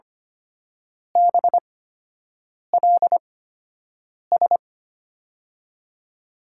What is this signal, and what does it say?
Morse code, heard as a keyed tone, about 25 wpm, spaced as for 7 wpm, 700 Hz. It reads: BLS
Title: USA Call Signs - 8x Character Spacing - 25wpm